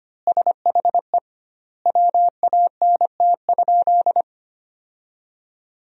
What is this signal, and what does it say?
Morse code, heard as a keyed tone, about 25 wpm, 700 Hz. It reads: SHE WANT?
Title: 2x Word-Spacing - Sentences from Top 200 Words - 25wpm